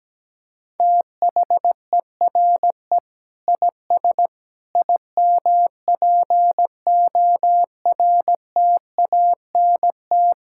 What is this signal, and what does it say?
Morse code, heard as a keyed tone, about 17 wpm, 700 Hz. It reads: THERE IS IMPORTANT
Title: Sentences from Top 1000 Words - Review Part 6 - 17wpm